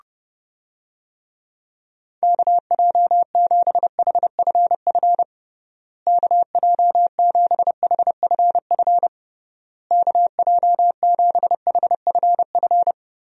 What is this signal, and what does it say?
Morse code, heard as a keyed tone, about 30 wpm, 700 Hz. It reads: KJ7HFF KJ7HFF KJ7HFF
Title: USA Call Signs - 6 Characters - Repeated 3x - 30wpm